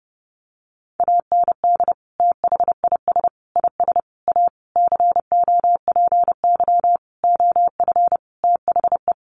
Morse code, heard as a keyed tone, about 30 wpm, 700 Hz. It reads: AND THIS IS A COPY OF THE
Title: Sentences from Top 1400 Words 30wpm